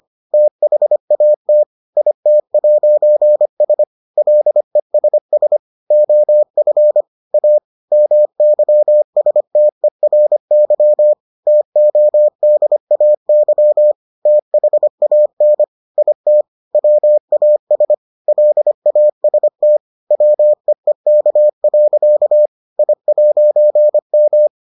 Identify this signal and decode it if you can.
Morse code, heard as a keyed tone, about 25 wpm, 600 Hz. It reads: THAT IT'S LESS OF A MYSTERY TODAY THAN IT WAS LAST WEEK. I'M